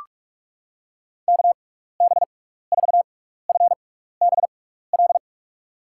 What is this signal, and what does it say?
Morse code, heard as a keyed tone, about 41 wpm, spaced as for 15 wpm, 700 Hz. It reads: KB4FBL
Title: USA Call Signs - 6 Characters - 6x Character Spacing - 45wpm